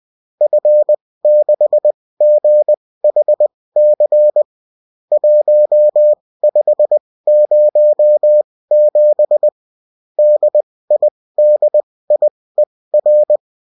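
Morse code, heard as a keyed tone, about 20 wpm, spaced as for 16 wpm, 600 Hz. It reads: F6GHC 1507 DIDIER
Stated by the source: Morse code bulletin of a French amateur CW union, sent as text and in CW.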